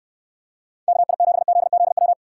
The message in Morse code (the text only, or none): KI7ZZW